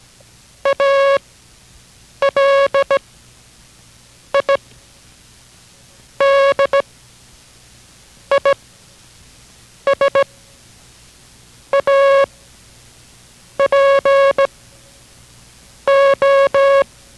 Morse code, heard as a keyed tone, about 17 wpm, 550 Hz. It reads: A L I D I S A P O